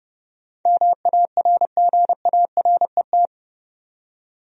Morse code, heard as a keyed tone, about 30 wpm, 700 Hz. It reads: MARGARET
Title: USA Common Names - 30wpm